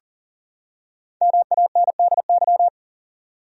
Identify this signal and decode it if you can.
Morse code, heard as a keyed tone, about 40 wpm, 700 Hz. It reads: MANDY
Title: USA Common Names - Max 5 Letters - 40wpm